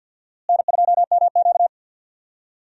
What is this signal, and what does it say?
Morse code, heard as a keyed tone, about 52 wpm, 700 Hz. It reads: N2MQ
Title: USA Call Signs - 4 Characters - Repeated 3x - 50wpm